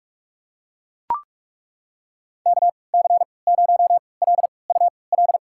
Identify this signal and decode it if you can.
Morse code, heard as a keyed tone, about 42 wpm, spaced as for 25 wpm, 700 Hz. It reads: KC0LUL
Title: USA Call Signs - 6 Characters - 3x Character Spacing - 45wpm